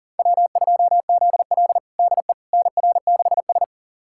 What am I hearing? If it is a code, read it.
Morse code, heard as a keyed tone, about 40 wpm, 700 Hz. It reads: W2ZL DE NR6S
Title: USA Callsign DE Callsign - 4 Characters - 40wpm